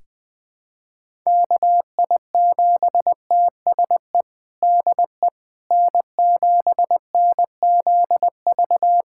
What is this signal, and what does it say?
Morse code, heard as a keyed tone, about 20 wpm, 700 Hz. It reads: KI7TSE DE N7NZV
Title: USA Callsign DE Callsign 20wpm